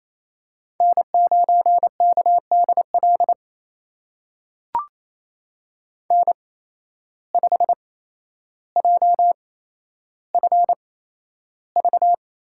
Morse code, heard as a keyed tone, about 28 wpm, spaced as for 12 wpm, 700 Hz. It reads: N9KDL N5JFV
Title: USA Call Signs - 5 Characters - 8x Character Spacing - 28wpm